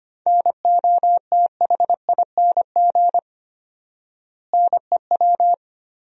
Morse code, heard as a keyed tone, about 25 wpm, 700 Hz. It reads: NOTHING NEW
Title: 4x Word-Spacing - Sentences from Top 500 Words - 25wpm